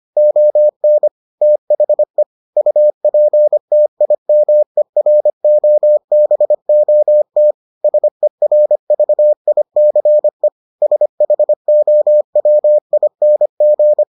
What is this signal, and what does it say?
Morse code, heard as a keyed tone, about 25 wpm, 600 Hz. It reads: ON THE UPTIMEROBOT SERVICE SHOWING